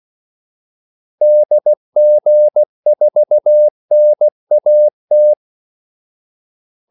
Morse code, heard as a keyed tone, about 16 wpm, 600 Hz. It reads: DG4NAT
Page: Kaldesignaler 16 wpm